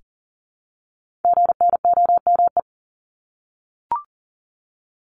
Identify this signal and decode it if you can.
Morse code, heard as a keyed tone, about 38 wpm, 700 Hz. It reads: GNOME
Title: Multi-letter Phonemes Repeated 3x 40wpm